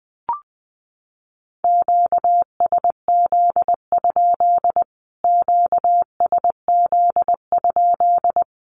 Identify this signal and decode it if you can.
Morse code, heard as a keyed tone, about 20 wpm, 700 Hz. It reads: QSZ? QSZ?
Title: Q-Codes Repeated 3x 20wpm